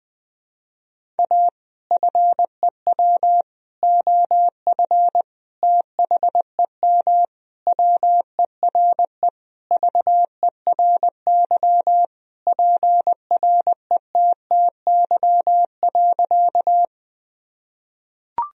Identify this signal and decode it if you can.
Morse code, heard as a keyed tone, about 20 wpm, 700 Hz. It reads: A FEW OF THEM WERE VERY PRETTY.
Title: Sentences from Top 1000 Words - Review Part 1 - 20wpm